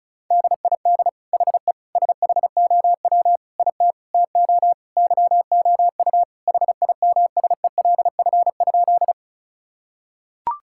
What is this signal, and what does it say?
Morse code, heard as a keyed tone, about 36 wpm, 700 Hz. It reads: DID HE SHOW IT TO YOU HIMSELF?
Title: Sentences from Top 1600 Words 35wpm